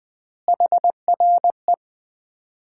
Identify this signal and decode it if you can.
Morse code, heard as a keyed tone, about 21 wpm, 700 Hz. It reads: HRE
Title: Common QSO Phrases - 3x Word-Spacing - 20wpm